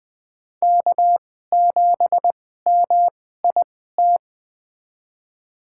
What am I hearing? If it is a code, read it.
Morse code, heard as a keyed tone, about 20 wpm, spaced as for 14 wpm, 700 Hz. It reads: K7MIT
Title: USA Call Signs - 2x Character Spacing - 20wpm